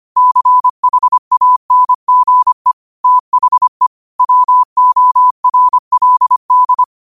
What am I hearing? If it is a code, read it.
Morse code, heard as a keyed tone, about 25 wpm, 1000 Hz. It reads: CHANGE THE WORLD
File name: qotdfiles / 25wpm